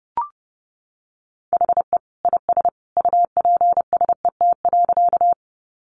Morse code, heard as a keyed tone, about 30 wpm, 700 Hz. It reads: HE IS UPSET.
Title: Additional Easy Phrases 30wpm